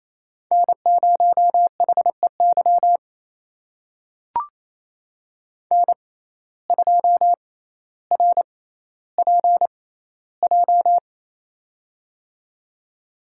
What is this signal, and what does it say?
Morse code, heard as a keyed tone, about 27 wpm, spaced as for 14 wpm, 700 Hz. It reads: N0HEY N2RPJ
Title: USA Call Signs - 5 Characters - 6x Character Spacing - 28wpm